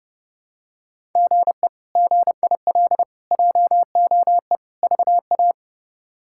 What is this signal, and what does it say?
Morse code, heard as a keyed tone, about 30 wpm, 700 Hz. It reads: GE GIL JOE VA
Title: K1USN SST QSO Practice - 30wpm